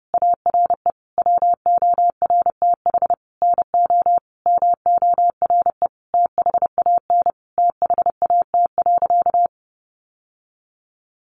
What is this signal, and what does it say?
Morse code, heard as a keyed tone, about 30 wpm, 700 Hz. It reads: ARE WORTH NO MORE THAN THAT.
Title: Sentences from Top 800 Words 30wpm